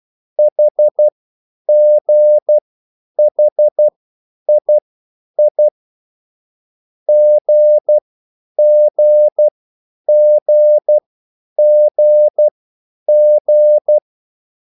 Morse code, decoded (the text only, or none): HGHII GGGGG